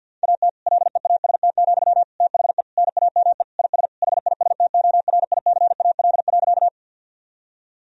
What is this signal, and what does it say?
Morse code, heard as a keyed tone, about 50 wpm, 700 Hz. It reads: AT LEAST, THE NAME IS HISTORICAL.
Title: Sentences from Top 1000 Words - Review Part 4 - 50wpm